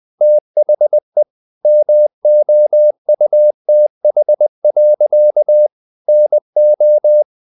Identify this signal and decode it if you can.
Morse code, heard as a keyed tone, about 20 wpm, 600 Hz. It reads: THE MOUTH. NO
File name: CW / Practice / EnglishSayings20